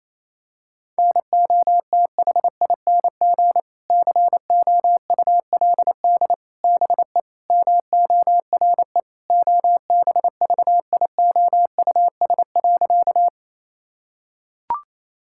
Morse code, heard as a keyed tone, about 28 wpm, 700 Hz. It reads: NOTHING COULD BE MORE OBVIOUS.
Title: Sentences from Top 1600 Words 28wpm